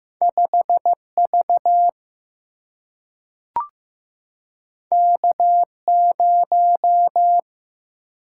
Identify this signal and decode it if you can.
Morse code, heard as a keyed tone, about 15 wpm, 700 Hz. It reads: 5V K0
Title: Single-Letter Single-Number 15wpm